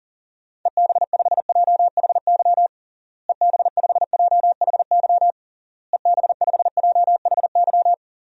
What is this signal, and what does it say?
Morse code, heard as a keyed tone, about 40 wpm, 700 Hz. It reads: EB5JHY EB5JHY EB5JHY
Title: DXCC-100 Call Signs - 6 Characters - Repeated 3x - 40wpm